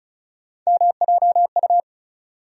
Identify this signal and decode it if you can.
Morse code, heard as a keyed tone, about 34 wpm, 700 Hz. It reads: MJU
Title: USA Call Sign Postfixes - 3 Letters 35wpm